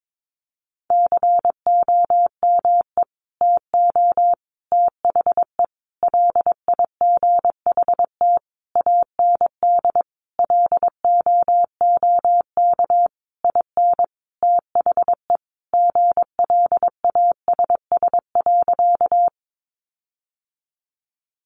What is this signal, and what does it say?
Morse code, heard as a keyed tone, about 22 wpm, 700 Hz. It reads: COME TO THE LIGHT AND LOOK IN THE GLASS.
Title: Sentences from Top 1300 Words 22wpm